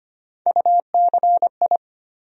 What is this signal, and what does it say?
Morse code, heard as a keyed tone, about 25 wpm, 700 Hz. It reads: UCI
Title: USA Call Sign Postfixes - 3 Letters 25wpm